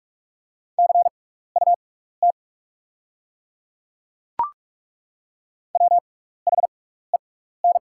Morse code, heard as a keyed tone, about 42 wpm, spaced as for 15 wpm, 700 Hz. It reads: CUT WHEN
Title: 6x Character Spacing - Top 500 Words - 45wpm